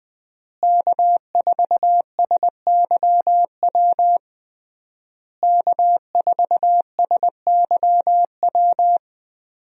Morse code, heard as a keyed tone, about 20 wpm, 700 Hz. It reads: K4SYW K4SYW
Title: USA Call Signs - 5 Characters - Repeated 3x - 20wpm